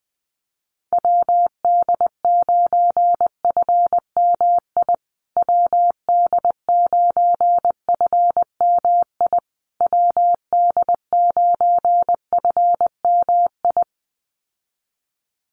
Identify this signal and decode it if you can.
Morse code, heard as a keyed tone, about 20 wpm, 700 Hz. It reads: WD9FMI WD9FMI WD9FMI
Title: USA Call Signs Repeated 3x 20wpm